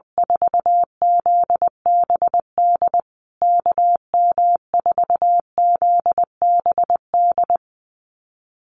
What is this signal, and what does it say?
Morse code, heard as a keyed tone, about 20 wpm, 700 Hz. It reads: 4ZBD KM4ZBD
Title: USA Call Signs Repeated 3x 20wpm